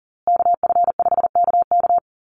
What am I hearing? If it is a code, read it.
Morse code, heard as a keyed tone, about 40 wpm, 700 Hz. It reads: KF5KK
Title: USA Call Signs Repeated 3x 40wpm